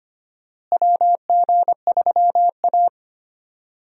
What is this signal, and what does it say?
Morse code, heard as a keyed tone, about 25 wpm, 700 Hz. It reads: WG3A
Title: USA Call Signs - 4 Characters - Repeated 3x - 25wpm